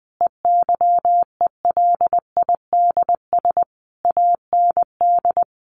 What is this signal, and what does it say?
Morse code, heard as a keyed tone, about 20 wpm, 700 Hz. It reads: EYELIDS AND
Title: A Sound Of Thunder 20wpm